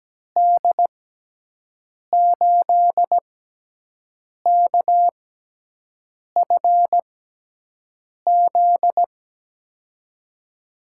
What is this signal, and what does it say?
Morse code, heard as a keyed tone, about 17 wpm, spaced as for 6 wpm, 700 Hz. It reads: D8KFZ
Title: USA Call Signs - 6 Characters - 6x Character Spacing - 17wpm